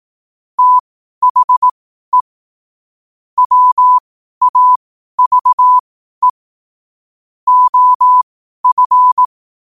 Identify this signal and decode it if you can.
Morse code, heard as a keyed tone, about 18 wpm, spaced as for 12 wpm, 1000 Hz. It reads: THE WAVE OF
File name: qotdfiles / 12wpm